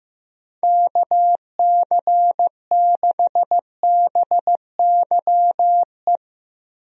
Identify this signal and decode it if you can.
Morse code, heard as a keyed tone, about 16 wpm, 700 Hz. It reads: KC6BYE
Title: USA Call Signs - 6 Characters - Repeated 3x - 15wpm